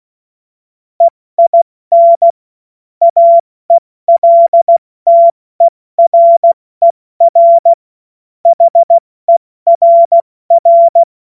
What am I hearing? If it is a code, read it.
Morse code, heard as a keyed tone, about 16 wpm, 680 Hz. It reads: EIN AELTERER HERR
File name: Morsekurs / Witze / Witz 23. Dezember 2025 70 F80